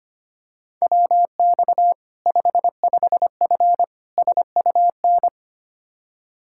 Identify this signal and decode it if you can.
Morse code, heard as a keyed tone, about 25 wpm, 700 Hz. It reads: WX 55F SUN
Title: Practice QSO - Exchange 2 - Info - 25wpm